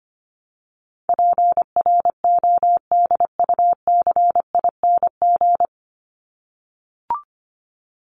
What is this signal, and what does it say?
Morse code, heard as a keyed tone, about 25 wpm, 700 Hz. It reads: PRODUCING
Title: ING Suffix - Top 100 Words with Base Words 25wpm